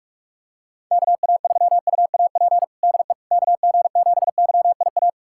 Morse code, heard as a keyed tone, about 47 wpm, 700 Hz. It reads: KA3UAP DE KG7YIA
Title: USA Callsign DE Callsign - 6 Characters - 45wpm